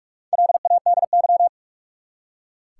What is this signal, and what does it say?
Morse code, heard as a keyed tone, about 46 wpm, 700 Hz. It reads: LADY
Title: Four Letter Words - 45wpm